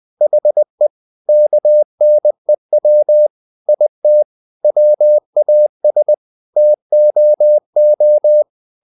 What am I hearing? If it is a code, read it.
Morse code, heard as a keyed tone, about 20 wpm, 600 Hz. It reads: HE KNEW IT WAS TOO
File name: the old man and the sea / the old man and the sea part 1 20wpm